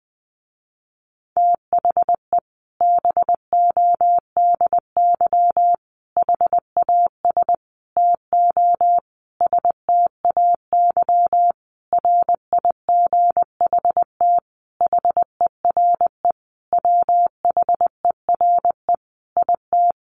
Morse code, heard as a keyed tone, about 20 wpm, 700 Hz. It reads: THE BODY HAS TO STAY RIGHT HERE WHERE IT
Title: A Sound Of Thunder 20wpm